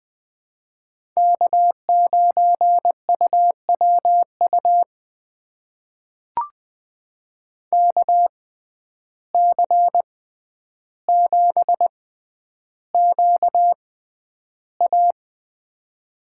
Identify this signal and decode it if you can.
Morse code, heard as a keyed tone, about 20 wpm, spaced as for 10 wpm, 700 Hz. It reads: K9UWU KC7QA
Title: USA Call Signs - 5 Characters - 6x Character Spacing - 20wpm